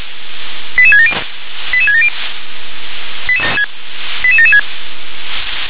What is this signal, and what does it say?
Five-tone selective call (ZVEI), heard as quick runs of five tones, tones of about 70 ms. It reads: 88670 88670 88776